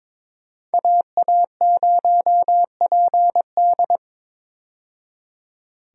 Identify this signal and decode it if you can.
Morse code, heard as a keyed tone, about 22 wpm, 700 Hz. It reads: AA0PD
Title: USA Call Signs - 5 Characters - 22wpm